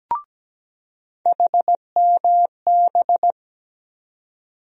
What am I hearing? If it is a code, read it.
Morse code, heard as a keyed tone, about 17 wpm, 700 Hz. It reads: HMB
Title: USA Call Sign Postfixes - 3 Letters 17wpm